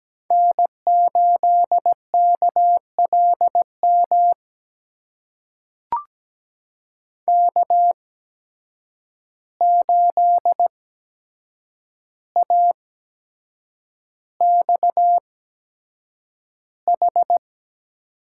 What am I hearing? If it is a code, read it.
Morse code, heard as a keyed tone, about 17 wpm, spaced as for 7 wpm, 700 Hz. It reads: N8KLM K8AXH
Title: USA Call Signs - 5 Characters - 8x Character Spacing - 17wpm